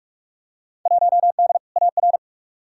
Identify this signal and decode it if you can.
Morse code, heard as a keyed tone, about 46 wpm, 700 Hz. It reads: 1D AR
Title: Field Day Exchanges - 45wpm